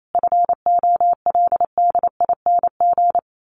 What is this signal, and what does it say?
Morse code, heard as a keyed tone, about 28 wpm, 700 Hz. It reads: FOLDING
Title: A Sound Of Thunder 28wpm